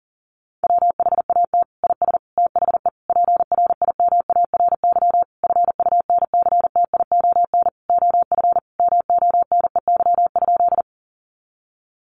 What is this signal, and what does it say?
Morse code, heard as a keyed tone, about 40 wpm, 700 Hz. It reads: WHAT IS THE PRIMARY FUNCTION OF MONEY?
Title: Sentences from Top 1100 Words 40wpm